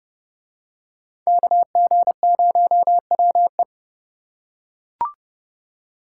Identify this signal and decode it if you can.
Morse code, heard as a keyed tone, about 30 wpm, 700 Hz. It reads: KG0WE